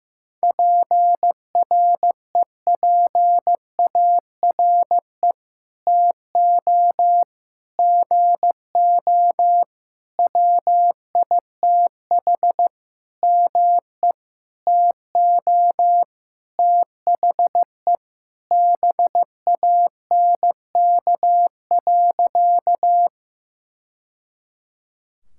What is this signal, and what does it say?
Morse code, heard as a keyed tone, about 15 wpm, 700 Hz. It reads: PREPARE TO GO WITH ME TO THE BANK.
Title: Sentences from Top 1000 Words - Review Part 6 - 15wpm